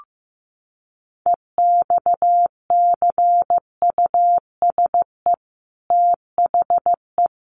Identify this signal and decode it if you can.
Morse code, heard as a keyed tone, about 15 wpm, 700 Hz. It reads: EXCUSE THE
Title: Sentences from Top 800 Words 15wpm